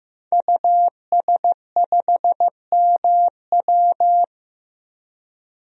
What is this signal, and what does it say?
Morse code, heard as a keyed tone, about 15 wpm, 700 Hz. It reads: US5MW
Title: DXCC-100 Call Signs - 5 Characters - 15wpm